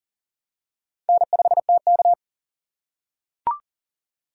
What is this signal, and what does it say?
Morse code, heard as a keyed tone, about 40 wpm, 700 Hz. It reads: N5TK
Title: USA Call Signs - 4 Characters - 4x Character Spacing - 40wpm